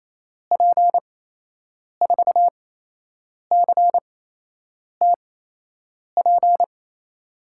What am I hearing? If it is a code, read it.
Morse code, heard as a keyed tone, about 27 wpm, spaced as for 8 wpm, 700 Hz. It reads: P4CTP
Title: USA Call Signs - 6 Characters - 8x Character Spacing - 28wpm